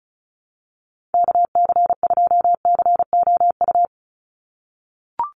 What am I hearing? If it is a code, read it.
Morse code, heard as a keyed tone, about 34 wpm, 700 Hz. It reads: KC2COU
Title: USA Call Signs 35wpm